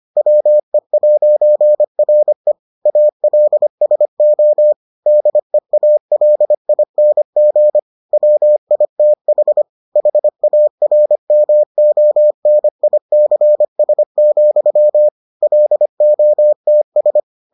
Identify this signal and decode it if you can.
Morse code, heard as a keyed tone, about 25 wpm, 600 Hz. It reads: WE'RE ALSO DEALING WITH HARMONICS, LOTS